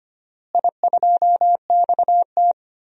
Morse code, heard as a keyed tone, about 25 wpm, 700 Hz. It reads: I2XT
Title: DXCC-100 Call Signs - 4 Characters - Repeated 2x - 25wpm